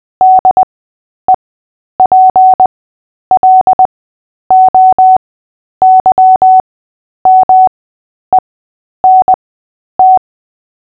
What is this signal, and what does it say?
Morse code, heard as a keyed tone, about 20 wpm, spaced as for 10 wpm, 750 Hz. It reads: DEPLOYMENT
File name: cw / fists / Short-Story-3-10wpm